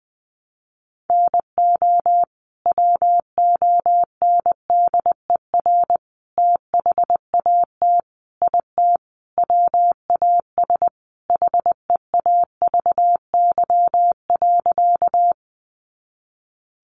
Morse code, heard as a keyed tone, about 20 wpm, 700 Hz. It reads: NO WONDER THAT IT WAS HEAVY.
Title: Sentences from Top 1100 Words 20wpm